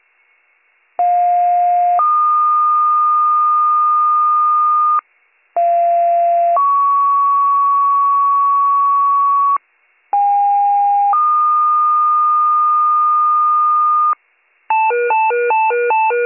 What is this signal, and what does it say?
Two-tone page, heard as a one-second tone, then a three-second tone, about 710 then 1220 Hz, about 690 then 1120 Hz, about 810 then 1220 Hz. It is heard from 1.0 to 14.1 s.